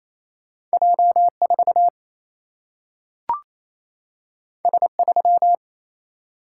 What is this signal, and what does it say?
Morse code, heard as a keyed tone, about 28 wpm, 700 Hz. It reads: J 4 S 3
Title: Single-Letter Single-Number 28wpm